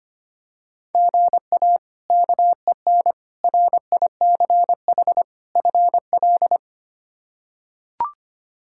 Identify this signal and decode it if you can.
Morse code, heard as a keyed tone, about 25 wpm, 700 Hz. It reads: GA KEN RICH FL
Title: K1USN SST QSO Practice - 25wpm